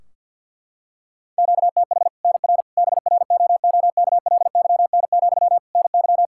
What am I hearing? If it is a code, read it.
Morse code, heard as a keyed tone, about 51 wpm, 700 Hz. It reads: QTH NR BROOKLYN, NY
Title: Practice QSO - Exchange 1 - Beginning - 50wpm